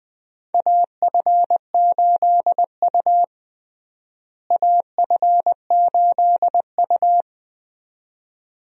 Morse code, heard as a keyed tone, about 20 wpm, 700 Hz. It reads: AF8U AF8U
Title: USA Call Signs - 4 Characters - Repeated 2x - 20wpm